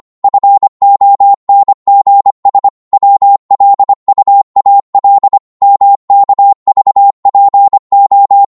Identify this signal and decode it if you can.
Morse code, heard as a keyed tone, about 25 wpm, 800 Hz. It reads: FONGS WLUAL MKVPO